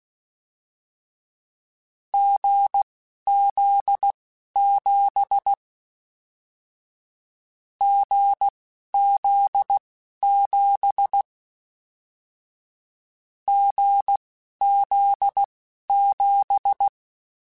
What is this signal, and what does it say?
Morse code, heard as a keyed tone, about 16 wpm, spaced as for 11 wpm, 800 Hz. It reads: GZ7 GZ7 GZ7
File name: Short Runs / Short Runs-mp3s / GZ GZ7 @10